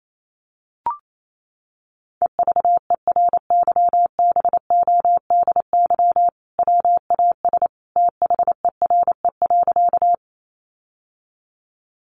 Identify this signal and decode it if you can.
Morse code, heard as a keyed tone, about 28 wpm, 700 Hz. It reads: EVERYBODY WAS THERE.
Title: Sentences from Top 1400 Words 28wpm